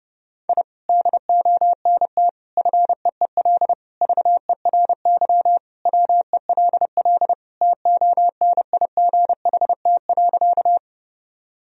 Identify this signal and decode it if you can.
Morse code, heard as a keyed tone, about 30 wpm, 700 Hz. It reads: I DONT FEEL VERY WELL TONIGHT.
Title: Sentences from Top 1000 Words - Review Part 3 - 30wpm